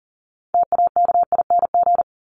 Morse code, heard as a keyed tone, about 40 wpm, 700 Hz. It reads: TAKING
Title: ING Suffix - Top 100 Words with Base Words 40wpm